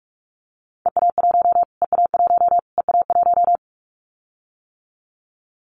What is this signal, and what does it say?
Morse code, heard as a keyed tone, about 44 wpm, 700 Hz. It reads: EA1 EA1 EA1
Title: Callsign Prefixes for Top 25 DXCC - Repeated 3x - 45wpm